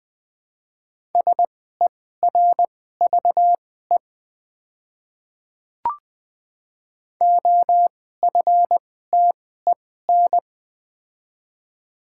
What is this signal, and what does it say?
Morse code, heard as a keyed tone, about 21 wpm, spaced as for 15 wpm, 700 Hz. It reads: SERVE OFTEN
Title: 2x Character Spacing - Top 500 Words - 20wpm